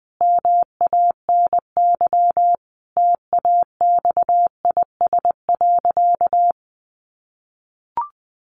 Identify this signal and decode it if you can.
Morse code, heard as a keyed tone, about 19 wpm, 700 Hz. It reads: MANY TAXIS.
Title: Easy Phrases 20wpm